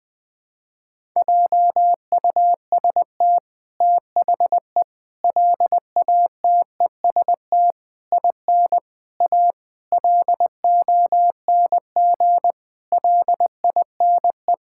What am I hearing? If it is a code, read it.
Morse code, heard as a keyed tone, about 21 wpm, 700 Hz. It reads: JUST THE LATEST IN A LONG LINE
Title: Sets of 8 Words - 20wpm